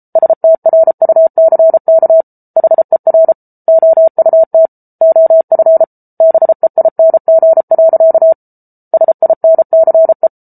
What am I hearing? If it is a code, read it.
Morse code, heard as a keyed tone, about 34 wpm, 650 Hz. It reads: STRUCK HER OUT OF BEING. SINCE